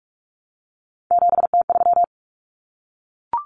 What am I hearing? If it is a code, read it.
Morse code, heard as a keyed tone, about 44 wpm, 700 Hz. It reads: 7T3
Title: Common Cut Numbers - 3-Digit 45wpm